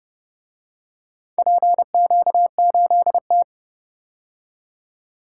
Morse code, heard as a keyed tone, about 29 wpm, 700 Hz. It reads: PQ8T